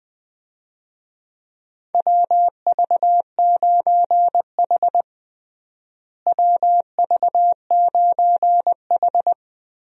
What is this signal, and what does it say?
Morse code, heard as a keyed tone, about 20 wpm, 700 Hz. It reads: WV9H WV9H
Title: USA Call Signs - 4 Characters - Repeated 3x - 20wpm